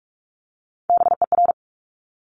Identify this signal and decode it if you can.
Morse code, heard as a keyed tone, about 46 wpm, 690 Hz. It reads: BER
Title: Top Three Letter Combinations in Words 45wpm